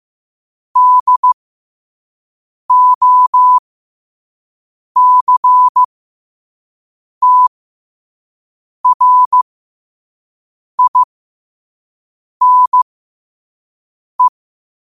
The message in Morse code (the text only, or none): DOCTRINE